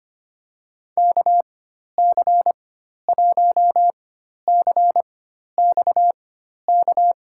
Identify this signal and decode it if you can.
Morse code, heard as a keyed tone, about 25 wpm, spaced as for 12 wpm, 700 Hz. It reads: KC1CXK
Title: USA Call Signs - 4x Character Spacing - 25wpm